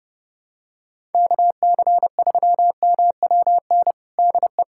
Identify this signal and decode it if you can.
Morse code, heard as a keyed tone, about 30 wpm, 700 Hz. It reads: KC3MWN DE
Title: USA Callsign DE Callsign - 6 Characters - 30wpm